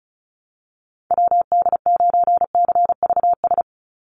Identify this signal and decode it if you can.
Morse code, heard as a keyed tone, about 34 wpm, 700 Hz. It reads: WD9CVS
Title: USA Call Signs 35wpm